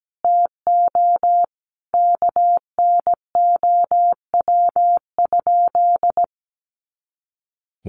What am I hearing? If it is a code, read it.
Morse code, heard as a keyed tone, about 17 wpm, 700 Hz. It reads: TO KNOW?
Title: Sentences from Top 200 Words 17wpm